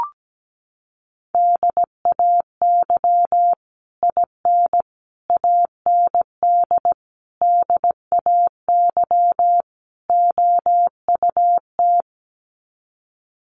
Morse code, heard as a keyed tone, about 17 wpm, 700 Hz. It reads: DAY IN AND DAY OUT
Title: Sets of 5 Words Encore2 17wpm